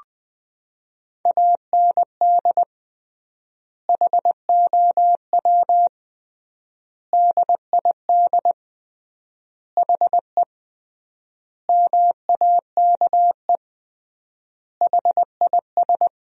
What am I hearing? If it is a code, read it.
Morse code, heard as a keyed tone, about 20 wpm, 700 Hz. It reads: AND HOW DID HE MAKE HIS
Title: 3x Word-Spacing - Sentences from Top 400 Words - 20wpm